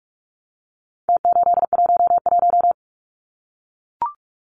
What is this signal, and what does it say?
Morse code, heard as a keyed tone, about 44 wpm, 700 Hz. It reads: T811
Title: Numbers - Contest Style - 4-Digit 45wpm